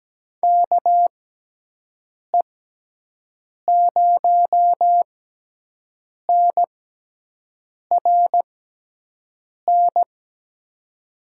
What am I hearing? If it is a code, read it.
Morse code, heard as a keyed tone, about 17 wpm, spaced as for 6 wpm, 700 Hz. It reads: KE0NRN
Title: USA Call Signs - 6x Character Spacing - 17wpm